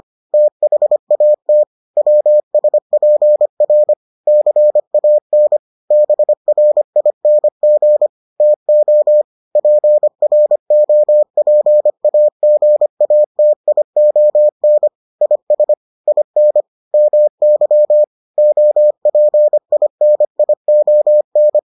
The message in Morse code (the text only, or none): THAT WSPR CAN BRING TO PROPAGATION IS IN MY OPINION